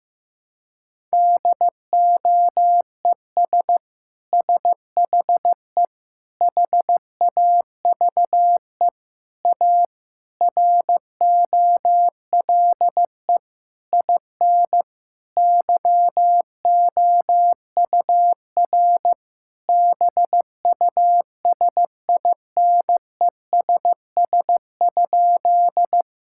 Morse code, heard as a keyed tone, about 15 wpm, 700 Hz. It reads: DOES SHE HAVE A ROLE IN YOUR BUSINESS?
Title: Sentences from Top 1000 Words - Review Part 6 - 15wpm